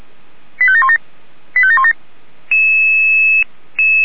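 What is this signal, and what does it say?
Five-tone selective call (ZVEI), heard as quick runs of five tones, tones of about 70 ms. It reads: 85717 85717
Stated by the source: ZVEI five-tone call received by SDR.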